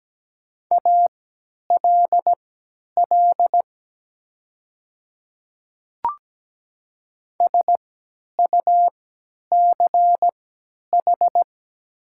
Morse code, heard as a keyed tone, about 17 wpm, spaced as for 10 wpm, 700 Hz. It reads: ALL SUCH
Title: 3x Character Spacing - Top 200 Words - 17wpm